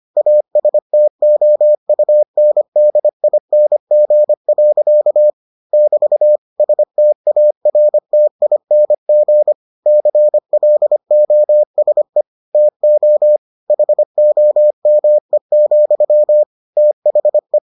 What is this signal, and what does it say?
Morse code, heard as a keyed tone, about 25 wpm, 600 Hz. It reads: ASTOUNDING. <BT> STARTING CLOSE TO HOME, THE